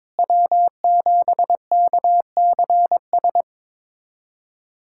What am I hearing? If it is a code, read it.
Morse code, heard as a keyed tone, about 21 wpm, 700 Hz. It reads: W7KCS
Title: USA Callsign DE Callsign - 5 Characters - 22wpm